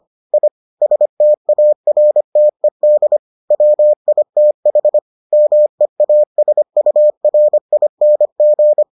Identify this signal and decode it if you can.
Morse code, heard as a keyed tone, about 25 wpm, 600 Hz. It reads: I STARTED WITH MEASURING